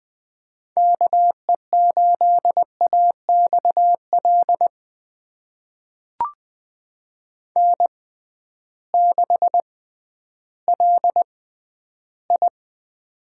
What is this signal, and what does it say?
Morse code, heard as a keyed tone, about 20 wpm, spaced as for 12 wpm, 700 Hz. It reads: KE8AXL N6LI
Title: USA Call Signs - 6x Character Spacing - 20wpm